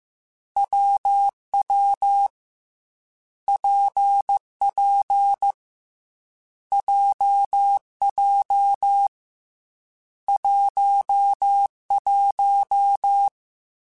Morse code, heard as a keyed tone, about 15 wpm, 800 Hz. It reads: WW PP JJ 11
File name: Difficult Combos / Difficult Combos-mp3s / WPJ WPJ1 @13